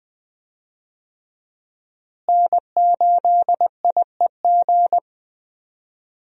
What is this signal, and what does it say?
Morse code, heard as a keyed tone, about 20 wpm, 700 Hz. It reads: N8IEG
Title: USA Call Signs - 5 Characters - Repeated 2x - 20wpm